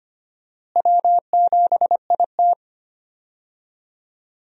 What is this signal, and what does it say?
Morse code, heard as a keyed tone, about 25 wpm, 700 Hz. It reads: W7IT
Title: USA Call Signs - 4 Characters - Repeated 3x - 25wpm